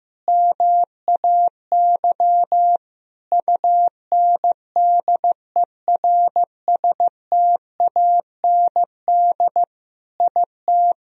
Morse code, heard as a keyed tone, about 15 wpm, 700 Hz. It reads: MAY UNDERSTAND IT
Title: Sentences from Top 1000 Words - Review Part 4 - 15wpm